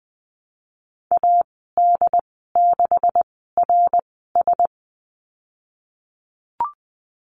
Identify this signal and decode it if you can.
Morse code, heard as a keyed tone, about 20 wpm, spaced as for 15 wpm, 700 Hz. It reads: AD6RS
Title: 154 Lesson 25g - Letter p - Call Signs 20wpm